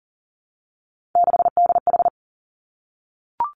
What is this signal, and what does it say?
Morse code, heard as a keyed tone, about 38 wpm, 700 Hz. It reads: 6DH
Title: Three Challenging Letter-Number Combinations 40wpm